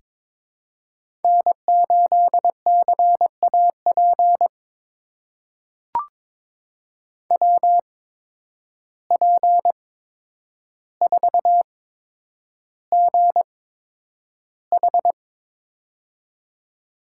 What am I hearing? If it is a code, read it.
Morse code, heard as a keyed tone, about 22 wpm, spaced as for 9 wpm, 700 Hz. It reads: N8CAP WP4GH
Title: USA Call Signs - 5 Characters - 8x Character Spacing - 22wpm